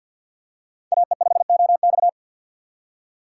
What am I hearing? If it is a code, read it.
Morse code, heard as a keyed tone, about 52 wpm, 700 Hz. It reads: AE5OX
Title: USA Call Signs - 5 Characters - 4x Character Spacing - 50wpm